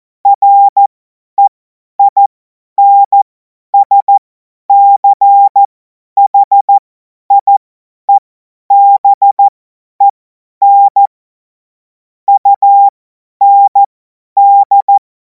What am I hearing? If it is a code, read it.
Morse code, heard as a keyed tone, about 14 wpm, spaced as for 10 wpm, 800 Hz. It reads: REINSCHIEBEN UND
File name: Morsekurs / Witze / Witz 23. Juni 2026 50 F70